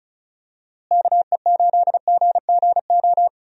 Morse code, heard as a keyed tone, about 34 wpm, 700 Hz. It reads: KE8GGO
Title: USA Call Signs - 6 Characters - Repeated 2x - 35wpm